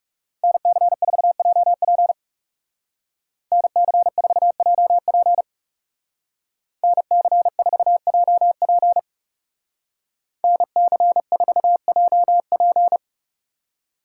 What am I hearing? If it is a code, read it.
Morse code, heard as a keyed tone, about 37 wpm, 700 Hz. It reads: NC4JP NC4JP NC4JP NC4JP
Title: Speed Racing - USA 2x2 Call Signs 30wpm